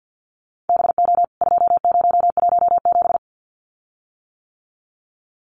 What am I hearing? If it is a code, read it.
Morse code, heard as a keyed tone, about 50 wpm, 700 Hz. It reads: BO 2017
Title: CWT Exchange 50wpm